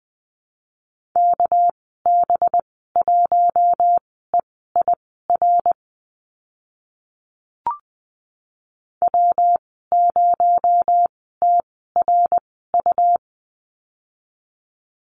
Morse code, heard as a keyed tone, about 20 wpm, spaced as for 14 wpm, 700 Hz. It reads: KB1EIR W0TRU
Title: 231 Lesson 37g - Letter x - Call Signs 20wpm